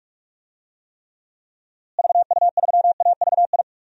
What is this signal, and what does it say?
Morse code, heard as a keyed tone, about 45 wpm, 700 Hz. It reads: VU3AVI
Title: DXCC-100 Call Signs - 6 Characters - Repeated 2x - 45wpm